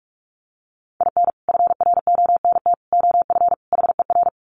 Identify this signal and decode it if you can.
Morse code, heard as a keyed tone, about 44 wpm, 700 Hz. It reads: IN FRONT OF HER
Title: Sets of 4 Words 45wpm